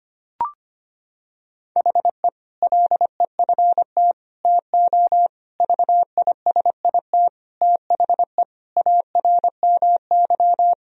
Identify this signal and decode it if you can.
Morse code, heard as a keyed tone, about 25 wpm, 700 Hz. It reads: HE LEFT TO VISIT THE ARMY
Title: Sentences from Top 1000 Words - Review Part 2 - 25wpm